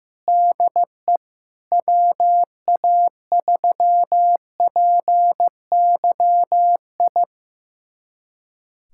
Morse code, heard as a keyed tone, about 15 wpm, 700 Hz. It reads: DE WA3PYI
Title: USA Callsign DE Callsign - 6 Characters - 15wpm